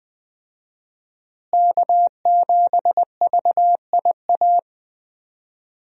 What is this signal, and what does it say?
Morse code, heard as a keyed tone, about 20 wpm, 700 Hz. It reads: K7VIA
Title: USA Call Signs - 5 Characters - Repeated 2x - 20wpm